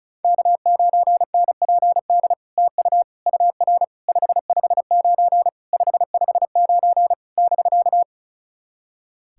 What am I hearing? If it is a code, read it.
Morse code, heard as a keyed tone, about 37 wpm, 700 Hz. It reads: K9NPD TU UR 559 559 <BK>
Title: SOTA QSO Practice 35wpm